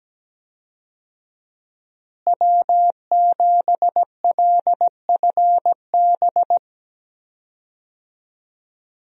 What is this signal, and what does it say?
Morse code, heard as a keyed tone, about 17 wpm, 700 Hz. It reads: W7LFB
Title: USA Call Signs - 5 Characters - Repeated 3x - 17wpm